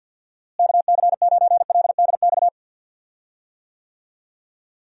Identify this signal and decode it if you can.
Morse code, heard as a keyed tone, about 46 wpm, 700 Hz. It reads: KC9LDX
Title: USA Call Signs - 6 Characters - Repeated 3x - 50wpm